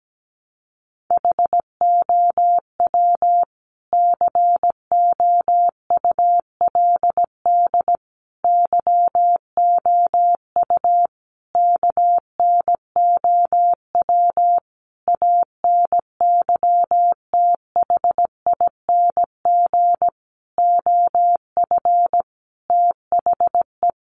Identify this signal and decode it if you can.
Morse code, heard as a keyed tone, about 17 wpm, 700 Hz. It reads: HOW COULD YOU KNOW ANYTHING OF THE
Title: Sentences from Top 1000 Words 17wpm